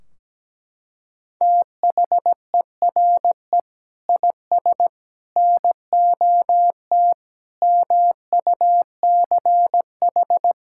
Morse code, heard as a keyed tone, about 17 wpm, 700 Hz. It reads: THERE IS NOT MUCH
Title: Sentences from Top 1600 Words 17wpm